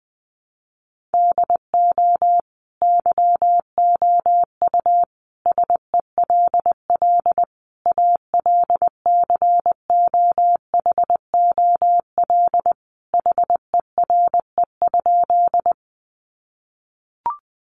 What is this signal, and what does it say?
Morse code, heard as a keyed tone, about 20 wpm, 700 Hz. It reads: DO YOU SELL ALCOHOL HERE?
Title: Sentences from Top 1400 Words 20wpm